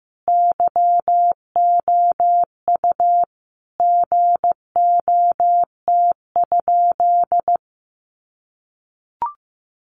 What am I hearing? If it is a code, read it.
Morse code, heard as a keyed tone, about 15 wpm, 700 Hz. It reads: YOU GOT?
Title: Sentences from Top 300 Words 15wpm